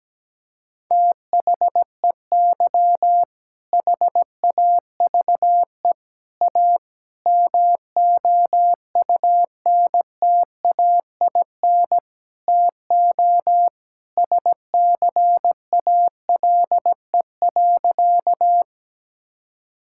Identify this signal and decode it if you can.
Morse code, heard as a keyed tone, about 17 wpm, 700 Hz. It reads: THEY HAVE A MOUNTAIN TO SCALE.
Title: Sentences from Top 1000 Words - Review Part 5 - 17wpm